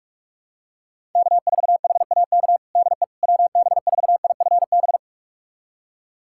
Morse code, heard as a keyed tone, about 46 wpm, 700 Hz. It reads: K4HAK DE WB4IFB
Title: USA Callsign DE Callsign 45wpm